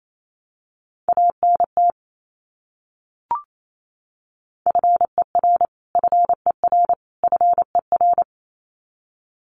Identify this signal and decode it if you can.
Morse code, heard as a keyed tone, about 28 wpm, 700 Hz. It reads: ANT FER FER FER
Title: Abbreviations Repeated 3x - 28wpm